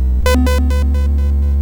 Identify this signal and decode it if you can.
There is no code.